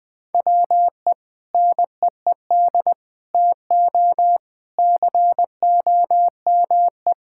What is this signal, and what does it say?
Morse code, heard as a keyed tone, about 20 wpm, 700 Hz. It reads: WE NEED TO COME
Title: Sentences with 7-Word Phrases - 20wpm